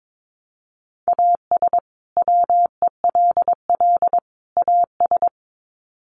Morse code, heard as a keyed tone, about 22 wpm, 700 Hz. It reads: AS WELL AS
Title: Sets of 3 Words 22wpm